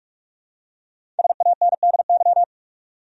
Morse code, heard as a keyed tone, about 47 wpm, 700 Hz. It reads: SANDY
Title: USA Common Names - Max 5 Letters - 45wpm